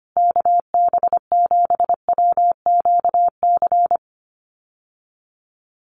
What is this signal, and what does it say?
Morse code, heard as a keyed tone, about 25 wpm, 700 Hz. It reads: KB7WQC
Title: USA Call Signs Repeated 2x 25wpm